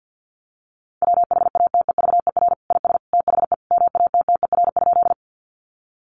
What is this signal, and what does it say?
Morse code, heard as a keyed tone, about 50 wpm, 700 Hz. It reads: WHATEVER IS THE MATTER?
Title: Sentences from Top 1100 Words 50wpm